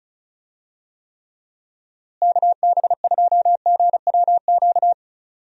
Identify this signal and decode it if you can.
Morse code, heard as a keyed tone, about 34 wpm, 700 Hz. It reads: KB2GWQ
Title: USA Call Signs - 6 Characters - Repeated 3x - 35wpm